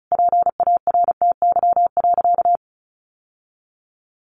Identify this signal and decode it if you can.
Morse code, heard as a keyed tone, about 34 wpm, 700 Hz. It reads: PARTY.
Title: Sentences from Top 1400 Words 35wpm